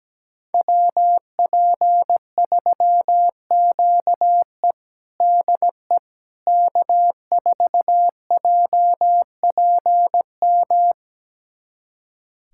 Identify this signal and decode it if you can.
Morse code, heard as a keyed tone, about 17 wpm, 700 Hz. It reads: WP3QE DE K4JPM
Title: USA Callsign DE Callsign - 5 Characters - 17wpm